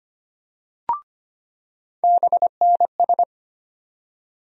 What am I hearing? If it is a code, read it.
Morse code, heard as a keyed tone, about 25 wpm, 700 Hz. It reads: BNS
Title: USA Call Sign Postfixes - 3 Letters 25wpm